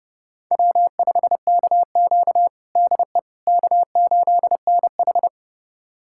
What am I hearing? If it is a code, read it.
Morse code, heard as a keyed tone, about 30 wpm, 700 Hz. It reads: W5KQ DE K8NH
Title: USA Callsign DE Callsign - 4 Characters - 30wpm